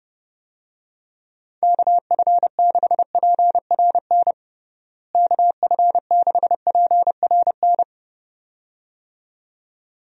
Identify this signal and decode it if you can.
Morse code, heard as a keyed tone, about 32 wpm, 700 Hz. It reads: KF6PRN KF6PRN
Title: USA Call Signs - 6 Characters - Repeated 2x - 30wpm